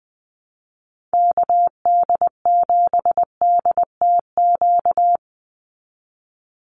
Morse code, heard as a keyed tone, about 20 wpm, 700 Hz. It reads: KD7DTQ
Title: USA Call Signs 20wpm